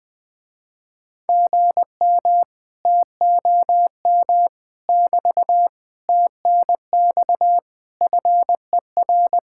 Morse code, heard as a keyed tone, about 19 wpm, 700 Hz. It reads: GM TOM <BT> TNX FER
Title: Speed-Racing Common QSO Phrases Encore - 20wpm